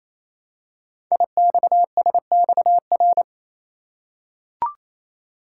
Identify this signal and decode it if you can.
Morse code, heard as a keyed tone, about 28 wpm, 700 Hz. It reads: IXSXR
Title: Code Groups - 5 Letter:Numbers - 28wpm